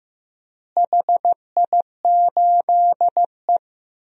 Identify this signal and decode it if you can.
Morse code, heard as a keyed tone, about 16 wpm, 700 Hz. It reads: HI8E
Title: DXCC-100 Call Signs - 4 Characters - Repeated 3x - 15wpm